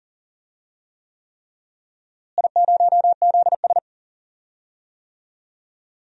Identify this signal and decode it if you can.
Morse code, heard as a keyed tone, about 40 wpm, 700 Hz. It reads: I0ZS
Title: DXCC-100 Call Signs - 4 Characters - Repeated 3x - 40wpm